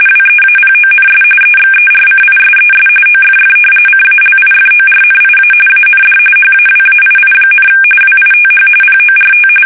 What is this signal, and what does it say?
RTTY, about 45.45 baud. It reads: WEAPONS EXERCISES OFF CALIFORNIA AND HAWAII, WITH PORT
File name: fsk850 / battleshipnj